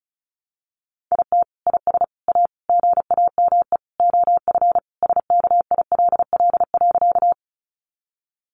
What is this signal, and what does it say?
Morse code, heard as a keyed tone, about 34 wpm, 700 Hz. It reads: IT IS A GAME OF SKILL.
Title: Sentences from Top 1200 Words 35wpm